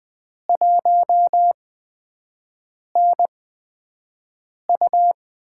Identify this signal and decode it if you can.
Morse code, heard as a keyed tone, about 20 wpm, spaced as for 5 wpm, 700 Hz. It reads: 1NU
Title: USA Call Signs - 5 Characters - 8x Character Spacing - 20wpm